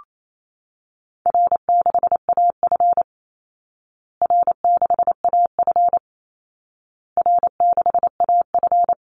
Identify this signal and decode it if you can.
Morse code, heard as a keyed tone, about 28 wpm, 700 Hz. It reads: R6AF R6AF R6AF
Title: Call Signs Repeated 2x - 28wpm